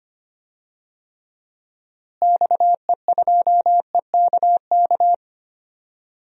Morse code, heard as a keyed tone, about 25 wpm, 700 Hz. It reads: XE2EKK
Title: DXCC-100 Call Signs - 6 Characters - Repeated 2x - 25wpm